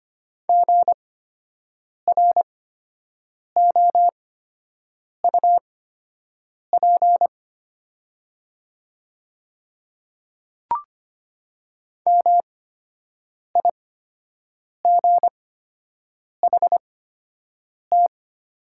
Morse code, heard as a keyed tone, about 25 wpm, spaced as for 7 wpm, 700 Hz. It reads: GROUP MIGHT
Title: 8x Character Spacing - Top 500 Words - 25wpm